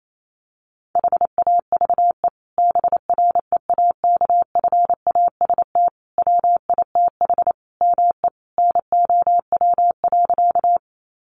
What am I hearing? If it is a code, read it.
Morse code, heard as a keyed tone, about 28 wpm, 700 Hz. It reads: HAVE BREAKFAST WITH ME NOW.
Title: Sentences from Top 1500 Words 28wpm